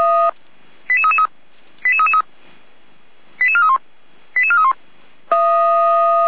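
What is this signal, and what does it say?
Five-tone selective call (ZVEI), heard as quick runs of five tones, tones of about 70 ms. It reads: 88333 88333 88431 88431
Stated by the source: ZVEI five-tone selective call received by SDR, logged by the fire brigade server feuerwehr-orlamuende.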